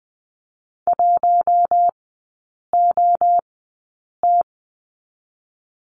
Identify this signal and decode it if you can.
Morse code, heard as a keyed tone, about 20 wpm, spaced as for 8 wpm, 700 Hz. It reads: 1OT
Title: 041 Lesson 8e - Number 1 - Three Characters 20wpm